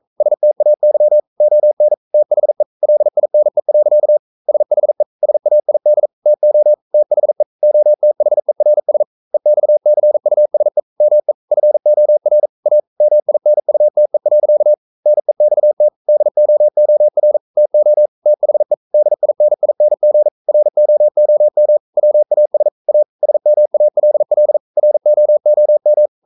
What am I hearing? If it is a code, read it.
Morse code, heard as a keyed tone, about 42 wpm, 600 Hz. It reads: STAY ON THE LINE. SHE SAID TO THE OTHERS EXCUSE ME FOR A MINUTE. NEXT DOOR TO THE DINING ROOM WAS A SMALL ROOM
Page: Romanen Never af Ken Follet på engelsk.